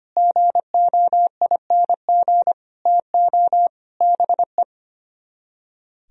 Morse code, heard as a keyed tone, about 33 wpm, 700 Hz. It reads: GOING TO BE